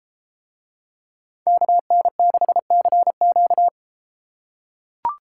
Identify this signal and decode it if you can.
Morse code, heard as a keyed tone, about 32 wpm, 700 Hz. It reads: KN6CQ